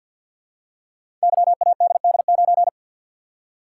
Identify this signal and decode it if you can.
Morse code, heard as a keyed tone, about 47 wpm, 700 Hz. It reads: YADD9
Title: Code Groups - 5 Letter:Numbers - 50wpm